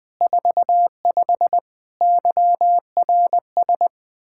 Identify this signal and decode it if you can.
Morse code, heard as a keyed tone, about 20 wpm, 700 Hz. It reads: 45 YRS
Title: K1USN SST QSO Practice 20wpm